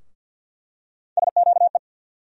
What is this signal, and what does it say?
Morse code, heard as a keyed tone, about 48 wpm, 690 Hz. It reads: SQE
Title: USA Call Sign Postfixes - 3 Letters 50wpm